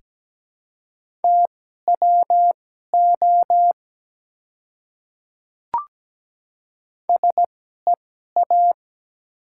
Morse code, heard as a keyed tone, about 17 wpm, spaced as for 12 wpm, 700 Hz. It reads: TWO SEA
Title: 2x Character Spacing - Top 300 Words - 17wpm